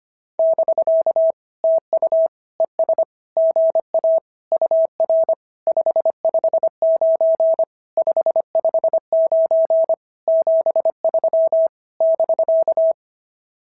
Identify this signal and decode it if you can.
Morse code, heard as a keyed tone, about 25 wpm, 650 Hz. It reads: <BK> TU ES GA UR 559 559 73 <BK>
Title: SOTA QSO Practice 25wpm